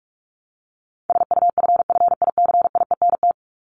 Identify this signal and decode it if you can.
Morse code, heard as a keyed tone, about 44 wpm, 700 Hz. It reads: SUFFICIENT
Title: Top 1251-1500 Words 45wpm